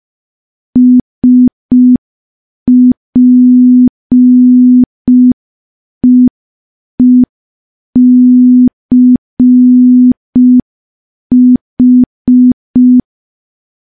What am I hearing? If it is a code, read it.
Morse code, heard as a keyed tone, about 5 wpm, 250 Hz. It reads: SPEECH
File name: morse / morse-training-05WPM